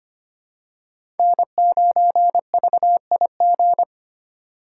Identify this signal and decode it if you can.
Morse code, heard as a keyed tone, about 25 wpm, 700 Hz. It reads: N9VIG
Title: USA Call Signs - 3x Character Spacing - 25wpm